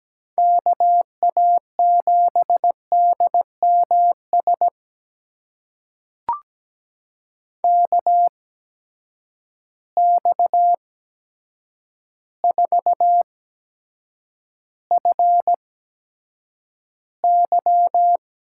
Morse code, heard as a keyed tone, about 17 wpm, spaced as for 8 wpm, 700 Hz. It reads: KA7DMS KX4FY
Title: USA Call Signs - 8x Character Spacing - 17wpm